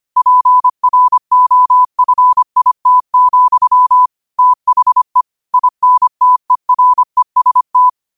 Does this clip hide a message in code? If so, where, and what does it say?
Yes: Morse code from 0.2 to 7.9 s, heard as a keyed tone, about 25 wpm, 1000 Hz. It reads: PROFIT, THE INTEREST